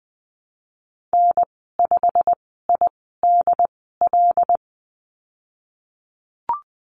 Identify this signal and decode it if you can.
Morse code, heard as a keyed tone, about 20 wpm, spaced as for 15 wpm, 700 Hz. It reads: N5IDL
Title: 173 Lesson 28g - Number 9 - Call Signs 20wpm